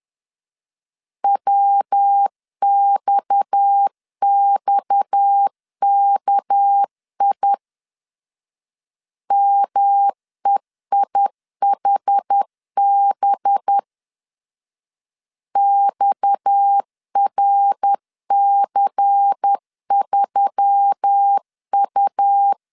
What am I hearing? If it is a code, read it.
Morse code, heard as a keyed tone, about 10 wpm, 800 Hz. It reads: WXXKI MEIHB XRC3U
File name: LEZIONI / 6 PKX0 / 6 PKX 10wpm 2426